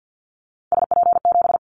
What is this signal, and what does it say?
Morse code, heard as a keyed tone, about 50 wpm, 700 Hz. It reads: SP7
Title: Callsign Prefixes for Top 25 DXCC - Repeated 3x - 50wpm